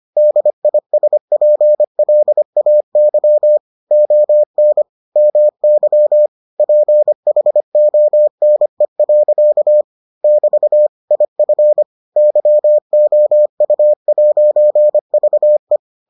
Morse code, heard as a keyed tone, about 25 wpm, 600 Hz. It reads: DISPLAY ON MY PHONE. <BT> IF YOU'VE